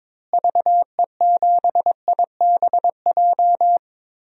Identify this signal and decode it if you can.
Morse code, heard as a keyed tone, about 22 wpm, 700 Hz. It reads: VE7IBJ
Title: DXCC-100 Call Signs - 6 Characters - Repeated 3x - 22wpm